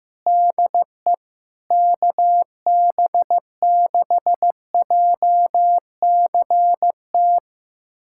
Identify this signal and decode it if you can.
Morse code, heard as a keyed tone, about 15 wpm, 700 Hz. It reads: DE KB6JCT
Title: USA Callsign DE Callsign - 6 Characters - 15wpm